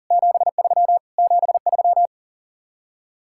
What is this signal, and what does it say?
Morse code, heard as a keyed tone, about 53 wpm, 700 Hz. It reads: 73 73